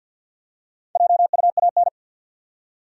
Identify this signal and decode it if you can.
Morse code, heard as a keyed tone, about 52 wpm, 700 Hz. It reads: JUAN
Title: USA Common Names - 50wpm